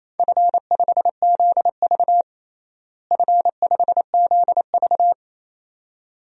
Morse code, heard as a keyed tone, about 27 wpm, 700 Hz. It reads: F5ZV F5ZV
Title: DXCC-100 Call Signs - 4 Characters - Repeated 2x - 28wpm